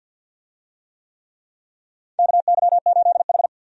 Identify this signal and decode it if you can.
Morse code, heard as a keyed tone, about 53 wpm, 700 Hz. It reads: KY8H
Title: USA Call Signs - 4 Characters - 50wpm